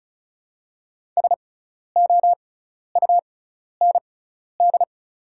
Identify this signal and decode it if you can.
Morse code, heard as a keyed tone, about 36 wpm, spaced as for 12 wpm, 700 Hz. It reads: SOUND
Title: 6x Character Spacing - Top 200 Words - 35wpm